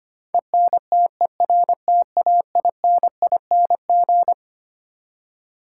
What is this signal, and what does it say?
Morse code, heard as a keyed tone, about 25 wpm, 700 Hz. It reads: ENTERTAINING
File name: Twelve Letter Words - 25wpm